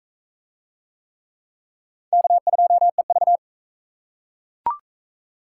Morse code, heard as a keyed tone, about 40 wpm, 700 Hz. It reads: K2EV